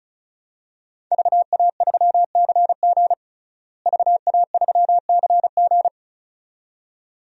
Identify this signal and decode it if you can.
Morse code, heard as a keyed tone, about 34 wpm, 700 Hz. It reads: VA3CG VA3CG
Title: DXCC-100 Call Signs - 5 Characters - Repeated 2x - 35wpm